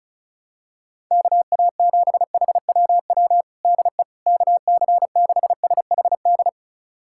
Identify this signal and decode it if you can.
Morse code, heard as a keyed tone, about 36 wpm, 700 Hz. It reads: KA7HWW DE KC6SHD
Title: USA Callsign DE Callsign 35wpm